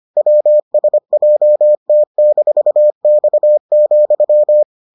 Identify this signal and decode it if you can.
Morse code, heard as a keyed tone, about 25 wpm, 600 Hz. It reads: WSJT-X,